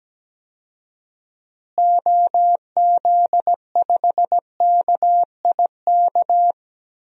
Morse code, heard as a keyed tone, about 17 wpm, 700 Hz. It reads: OZ5KIK
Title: DXCC-100 Call Signs - 6 Characters - Repeated 2x - 17wpm